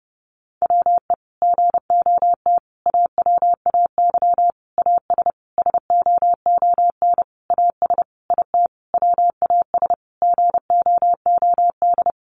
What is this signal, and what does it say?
Morse code, heard as a keyed tone, about 30 wpm, 700 Hz. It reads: WE GOT AWAY AS SOON AS IT WAS GOOD
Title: Sentences from Top 600 Words 30wpm